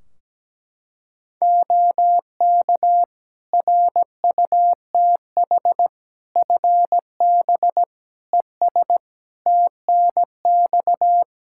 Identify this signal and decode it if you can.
Morse code, heard as a keyed tone, about 17 wpm, 700 Hz. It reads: OK RUTH FB ES TNX
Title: Practice QSO - Exchange 2 - Info - 17wpm